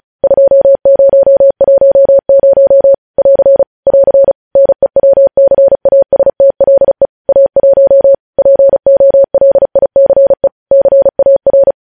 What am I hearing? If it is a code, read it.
Morse code, heard as a keyed tone, about 36 wpm, 550 Hz. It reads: 2010 <AR> <AR> NEWCASTLE A1 POLICE CAR
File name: podcast / News headlines-1x-35x35